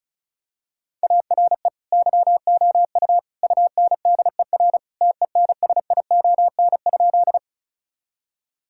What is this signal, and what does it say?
Morse code, heard as a keyed tone, about 36 wpm, 700 Hz. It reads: ARE YOU UNDER TENSION?
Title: Sentences from Top 1600 Words 35wpm